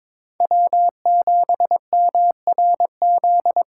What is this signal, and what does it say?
Morse code, heard as a keyed tone, about 26 wpm, 700 Hz. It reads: W7MRZ
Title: Speed Racing - USA 1x3 Call Signs 22wpm